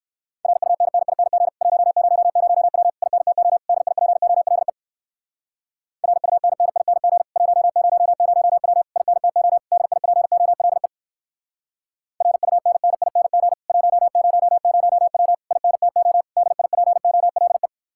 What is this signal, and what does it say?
Morse code, heard as a keyed tone, about 59 wpm, 700 Hz. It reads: RUNNING 100W INTO DIPOLE RUNNING 100W INTO DIPOLE RUNNING 100W INTO DIPOLE